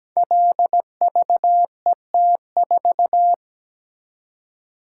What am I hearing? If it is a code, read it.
Morse code, heard as a keyed tone, about 17 wpm, 700 Hz. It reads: LVET4
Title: Code Groups - 5 Letter:Numbers - 17wpm